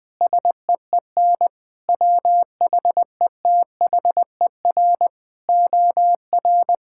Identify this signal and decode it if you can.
Morse code, heard as a keyed tone, about 20 wpm, 700 Hz. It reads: SEEN WHETHER OR
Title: Sets of 8 Words - 20wpm